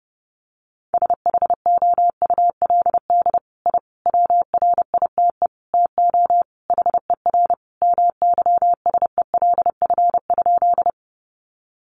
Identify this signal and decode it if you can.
Morse code, heard as a keyed tone, about 28 wpm, 700 Hz. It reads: SHOULD I WRITE TO HER MYSELF?
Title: Sentences from Top 1300 Words 30wpm